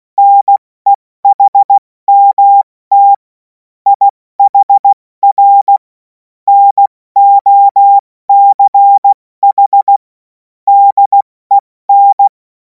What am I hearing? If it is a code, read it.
Morse code, heard as a keyed tone, about 16 wpm, 800 Hz. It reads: NEHMT IHR NOCH DEN
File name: Morsekurs / Witze / Witz 23. Juni 2026 70 F80